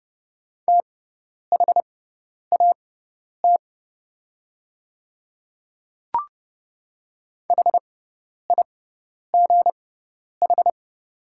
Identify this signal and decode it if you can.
Morse code, heard as a keyed tone, about 30 wpm, spaced as for 10 wpm, 700 Hz. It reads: THAT HIGH
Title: 6x Character Spacing - Top 300 Words - 30wpm